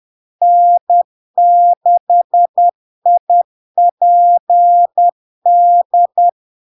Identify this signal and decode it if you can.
Morse code, heard as a keyed tone, about 10 wpm, 700 Hz. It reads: N6IPD
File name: LARK-FD-2022-Log-in-CW-10wpm-700-Hz